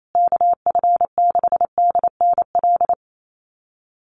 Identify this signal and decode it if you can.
Morse code, heard as a keyed tone, about 28 wpm, 700 Hz. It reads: KF6DNL
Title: USA Call Signs Repeated 3x 28wpm